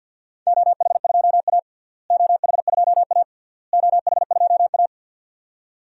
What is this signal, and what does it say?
Morse code, heard as a keyed tone, about 52 wpm, 700 Hz. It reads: OH2A OH2A OH2A
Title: DXCC-100 Call Signs - 4 Characters - Repeated 3x - 50wpm